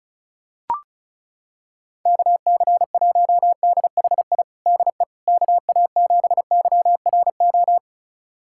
Morse code, heard as a keyed tone, about 34 wpm, 700 Hz. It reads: KC1DHI DE KA7YRO
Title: USA Callsign DE Callsign - 6 Characters - 35wpm